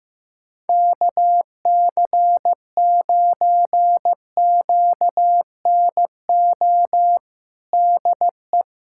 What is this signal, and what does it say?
Morse code, heard as a keyed tone, about 15 wpm, 700 Hz. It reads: KC9QNO DE
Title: USA Callsign DE Callsign 15wpm